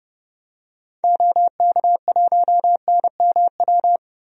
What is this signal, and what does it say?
Morse code, heard as a keyed tone, about 30 wpm, 700 Hz. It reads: OK1NMW
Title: DXCC-100 Call Signs - 6 Characters - Repeated 3x - 30wpm